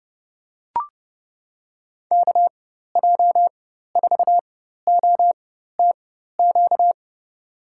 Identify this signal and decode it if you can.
Morse code, heard as a keyed tone, about 30 wpm, spaced as for 14 wpm, 700 Hz. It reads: KJ4OTQ
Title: USA Call Signs - 6 Characters - 4x Character Spacing - 30wpm